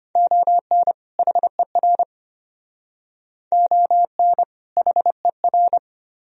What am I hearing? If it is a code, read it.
Morse code, heard as a keyed tone, about 34 wpm, 700 Hz. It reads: ON HER ON HER